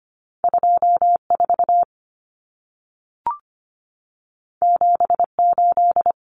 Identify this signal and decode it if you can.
Morse code, heard as a keyed tone, about 25 wpm, 700 Hz. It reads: 24 78
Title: Numbers - 2-Digit 25wpm